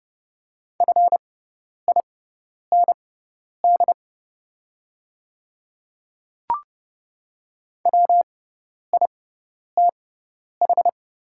Morse code, heard as a keyed tone, about 30 wpm, spaced as for 10 wpm, 690 Hz. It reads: FIND WITH
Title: 6x Character Spacing - Top 100 Words - 30wpm